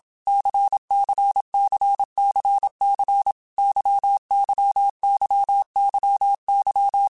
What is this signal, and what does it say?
Morse code, heard as a keyed tone, about 27 wpm, 800 Hz. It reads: CCCCC YYYYY
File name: Basic Letter Groups By7Groups / LFCYQ / LFCYQ@25